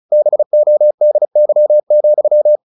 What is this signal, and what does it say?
Morse code, heard as a keyed tone, about 34 wpm, 600 Hz. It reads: BODY,
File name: the old man and the sea 2 35wpm